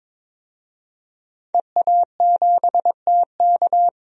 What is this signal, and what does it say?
Morse code, heard as a keyed tone, about 22 wpm, 700 Hz. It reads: EA7TK
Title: DXCC-100 Call Signs - 5 Characters - 22wpm